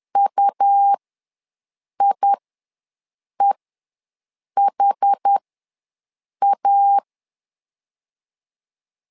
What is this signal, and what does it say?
Morse code, heard as a keyed tone, about 10 wpm, spaced as for 6 wpm, 800 Hz. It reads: UIEHA
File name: LEZIONI / 2 UHDN / 2 UHD ws 10wpm 2058